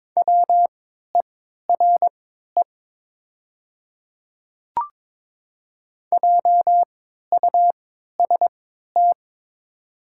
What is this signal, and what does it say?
Morse code, heard as a keyed tone, about 22 wpm, spaced as for 13 wpm, 700 Hz. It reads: WERE JUST
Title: 3x Character Spacing - Top 200 Words - 22wpm